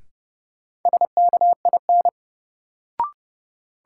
Morse code, heard as a keyed tone, about 30 wpm, 700 Hz. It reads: SKIN
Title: Sentences from Top 1300 Words - New Word Drill Alt - 30wpm